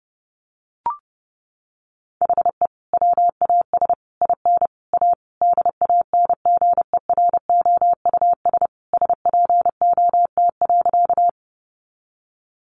Morse code, heard as a keyed tone, about 30 wpm, 700 Hz. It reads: HE WAS IN A DANGEROUS SPOT.
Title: Sentences from Top 1200 Words 30wpm